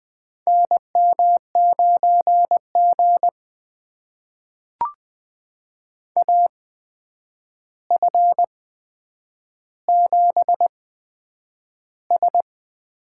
Morse code, heard as a keyed tone, about 20 wpm, spaced as for 9 wpm, 700 Hz. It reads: NM9G AF7S
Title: USA Call Signs - 4 Characters - 8x Character Spacing - 20wpm